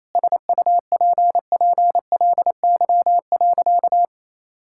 Morse code, heard as a keyed tone, about 27 wpm, 700 Hz. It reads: SUPPLY.
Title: Sentences from Top 1000 Words - Review Part 5 - 28wpm